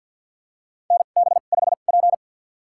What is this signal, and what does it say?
Morse code, heard as a keyed tone, about 53 wpm, spaced as for 37 wpm, 700 Hz. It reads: NB5P
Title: USA Call Signs - 4 Characters - 2x Character Spacing - 50wpm